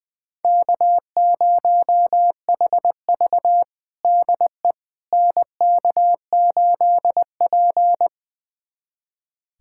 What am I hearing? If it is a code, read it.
Morse code, heard as a keyed tone, about 21 wpm, 700 Hz. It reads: K0HV DE NK8P
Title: USA Callsign DE Callsign - 4 Characters - 20wpm